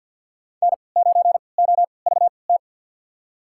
Morse code, heard as a keyed tone, about 53 wpm, spaced as for 29 wpm, 700 Hz. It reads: N9OVT
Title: USA Call Signs - 5 Characters - 3x Character Spacing - 50wpm